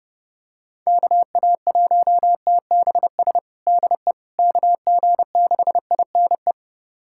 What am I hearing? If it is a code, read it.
Morse code, heard as a keyed tone, about 32 wpm, 700 Hz. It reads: KA1TBS DE KG6INE
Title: USA Callsign DE Callsign 30wpm